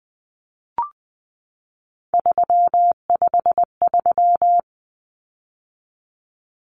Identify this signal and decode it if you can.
Morse code, heard as a keyed tone, about 20 wpm, 700 Hz. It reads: 353
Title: Common Cut Numbers - 3-Digit 20wpm